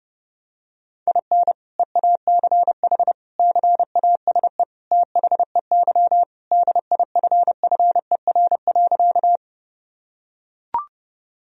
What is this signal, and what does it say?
Morse code, heard as a keyed tone, about 30 wpm, 700 Hz. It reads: IN EACH CASE THEY DIFFER.
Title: Sentences from Top 1000 Words - Review Part 1 - 30wpm